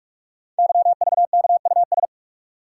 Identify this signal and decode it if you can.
Morse code, heard as a keyed tone, about 44 wpm, 690 Hz. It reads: YVKUS
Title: Code Groups - 5 Letters - 45wpm